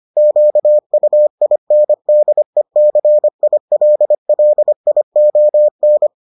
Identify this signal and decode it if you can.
Morse code, heard as a keyed tone, about 25 wpm, 600 Hz. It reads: QUINDECILLION